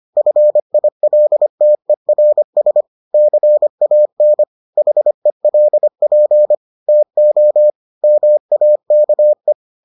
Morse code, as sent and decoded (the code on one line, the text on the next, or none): ..-. .. .-.. - . .-. ... / -.-. .- -. / .... .-.. .--. / - --- / -- .- -.- .
FILTERS CAN HELP TO MAKE